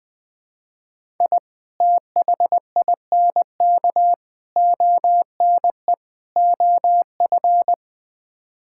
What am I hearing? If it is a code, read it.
Morse code, heard as a keyed tone, about 20 wpm, 700 Hz. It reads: I THINK ONE OF
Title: Build A Phrase - 5-Word Phrases - 20wpm